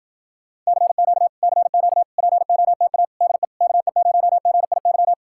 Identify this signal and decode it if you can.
Morse code, heard as a keyed tone, about 60 wpm, 700 Hz. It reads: CQ CQ POTA DE KE0GIY